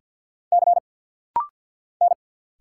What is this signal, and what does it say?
Morse code, heard as a keyed tone, about 50 wpm, 700 Hz. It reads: C N
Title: Single Letters - Rapid-Fire 50wpm